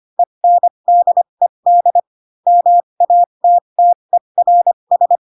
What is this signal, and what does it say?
Morse code, heard as a keyed tone, about 25 wpm, spaced as for 22 wpm, 700 Hz. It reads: ENDED MATTERS
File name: Pinocchio / Pinocchio ch3 18 25 ik1hge